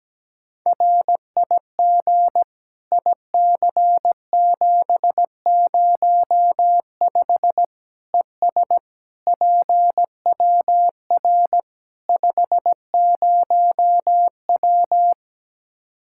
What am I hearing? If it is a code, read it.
Morse code, heard as a keyed tone, about 17 wpm, 700 Hz. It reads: RIG IC705 ES PWR 50W
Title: Practice QSO - Exchange 2 - Info - 17wpm